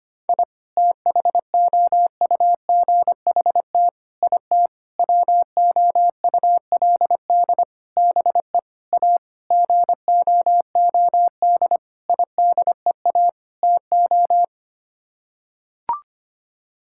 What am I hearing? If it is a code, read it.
Morse code, heard as a keyed tone, about 25 wpm, 700 Hz. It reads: I THOUGHT IT WOULD BE A GOOD IDEA TO
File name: Sets of 9 Words - 25wpm